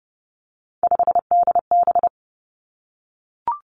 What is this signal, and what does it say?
Morse code, heard as a keyed tone, about 30 wpm, 700 Hz. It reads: 5DB
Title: Three Challenging Letter-Number Combinations 30wpm